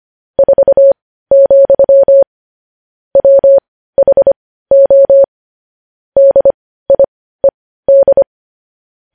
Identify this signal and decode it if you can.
Morse code, heard as a keyed tone, about 25 wpm, spaced as for 15 wpm, 550 Hz. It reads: 4, WHO DIED
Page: news-headlines-in-morse-code-at-15-wpm